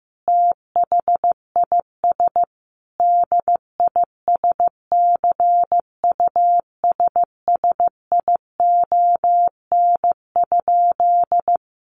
Morse code, heard as a keyed tone, about 15 wpm, 700 Hz. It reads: THIS DISCUSSION?
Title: Sentences from Top 1100 Words 15wpm